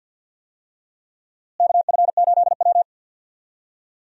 Morse code, heard as a keyed tone, about 52 wpm, 700 Hz. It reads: KF8W
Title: USA Call Signs - 4 Characters - Repeated 3x - 50wpm